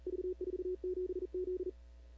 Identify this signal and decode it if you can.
Morse code, heard as a keyed tone, about 38 wpm, 410 Hz. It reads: V47Z
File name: CW / 2005-k5tr arrl cw dx -15:00